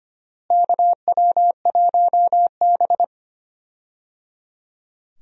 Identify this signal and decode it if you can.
Morse code, heard as a keyed tone, about 25 wpm, 700 Hz. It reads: KW1B
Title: USA Call Signs - 4 Characters - Repeated 3x - 25wpm